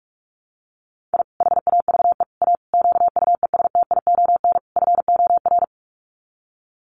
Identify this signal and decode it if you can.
Morse code, heard as a keyed tone, about 44 wpm, 700 Hz. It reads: I HAVE A QUESTION FOR
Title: Sets of 5 Words Encore2 45wpm